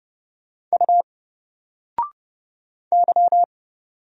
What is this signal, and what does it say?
Morse code, heard as a keyed tone, about 30 wpm, 700 Hz. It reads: U Y
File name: Single-Letter-Number-30wpm